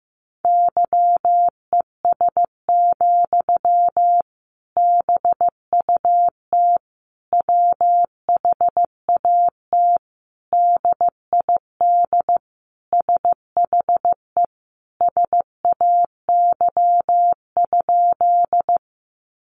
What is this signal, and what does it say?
Morse code, heard as a keyed tone, about 15 wpm, 700 Hz. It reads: YES, BUT WHAT DID SHE SAY?
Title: Sentences from Top 500 Words 15wpm